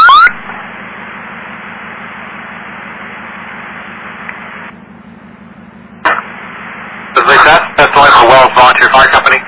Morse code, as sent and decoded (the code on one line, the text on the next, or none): none